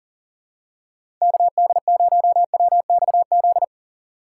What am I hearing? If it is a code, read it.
Morse code, heard as a keyed tone, about 40 wpm, 700 Hz. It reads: KD0WXZ
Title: USA Call Signs - 6 Characters - 2x Character Spacing - 40wpm